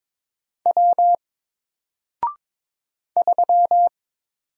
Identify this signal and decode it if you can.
Morse code, heard as a keyed tone, about 22 wpm, 700 Hz. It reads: W 3
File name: Single-Letter-Number-22wpm